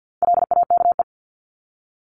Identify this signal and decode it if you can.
Morse code, heard as a keyed tone, about 50 wpm, 700 Hz. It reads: LAKE
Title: Top 1001-1250 Words 50wpm